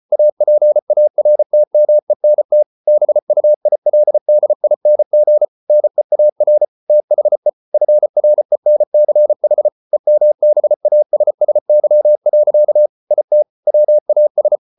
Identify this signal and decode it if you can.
Morse code, heard as a keyed tone, about 34 wpm, 600 Hz. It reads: APARTMENT BUILDING NEAR THE FRENCH EMBASSY. IT WAS